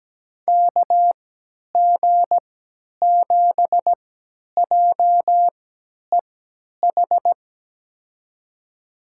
Morse code, heard as a keyed tone, about 17 wpm, spaced as for 10 wpm, 700 Hz. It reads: KG7JEH
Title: USA Call Signs - 6 Characters - 3x Character Spacing - 17wpm